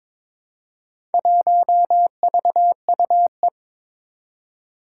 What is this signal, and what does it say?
Morse code, heard as a keyed tone, about 22 wpm, 700 Hz. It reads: 1VUE
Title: Four Challenging Letter-Number Combinations 22wpm